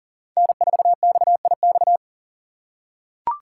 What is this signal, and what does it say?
Morse code, heard as a keyed tone, about 37 wpm, 700 Hz. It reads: N4XIX
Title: USA Call Signs - 5 Characters - 2x Character Spacing - 40wpm